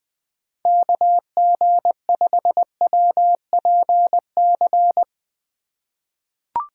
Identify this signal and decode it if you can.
Morse code, heard as a keyed tone, about 20 wpm, 700 Hz. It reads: KG5WPC
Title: USA Call Signs - 6 Characters - 4x Character Spacing - 20wpm